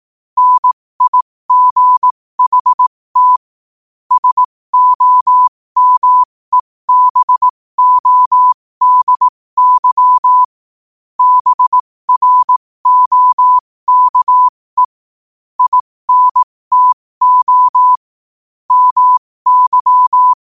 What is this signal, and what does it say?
Morse code, heard as a keyed tone, about 18 wpm, spaced as for 15 wpm, 1000 Hz. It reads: NIGHT SOMEBODY BROKE INTO MY
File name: qotdfiles / 15wpm